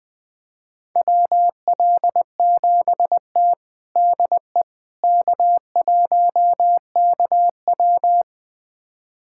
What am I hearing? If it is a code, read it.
Morse code, heard as a keyed tone, about 20 wpm, 700 Hz. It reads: WL7T DE K1KW
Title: USA Callsign DE Callsign - 4 Characters - 20wpm